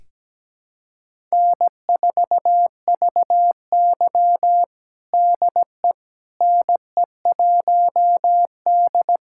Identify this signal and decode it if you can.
Morse code, heard as a keyed tone, about 17 wpm, 700 Hz. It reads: N4VY DE NE1D
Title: USA Callsign DE Callsign - 4 Characters - 17wpm